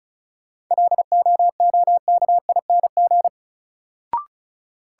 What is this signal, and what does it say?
Morse code, heard as a keyed tone, about 36 wpm, 700 Hz. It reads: LOOKING
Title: Sentences from Top 1000 Words - Review Part 6 - New Word Drill - 35wpm